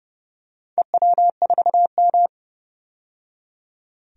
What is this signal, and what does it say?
Morse code, heard as a keyed tone, about 30 wpm, 700 Hz. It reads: EW4M
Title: DXCC-100 Call Signs - 4 Characters - 30wpm